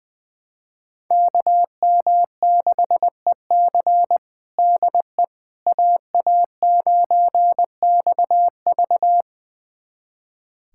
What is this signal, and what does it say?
Morse code, heard as a keyed tone, about 20 wpm, 700 Hz. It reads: KM6EC DE AA9XV
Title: USA Callsign DE Callsign - 5 Characters - 20wpm